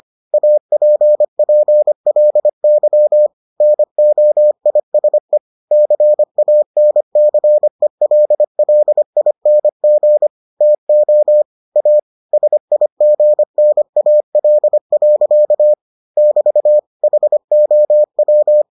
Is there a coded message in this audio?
Yes: Morse code from 0.3 to 18.6 s, heard as a keyed tone, about 25 wpm, 600 Hz. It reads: APPLY NOISE CANCELLING TO A SIGNAL. <BT> HOW